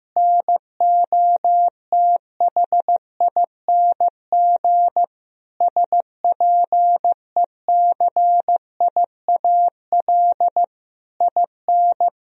Morse code, heard as a keyed tone, about 15 wpm, 700 Hz. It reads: NOTHING SPECIAL IN
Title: Sentences from Top 1000 Words - Review Part 3 - 15wpm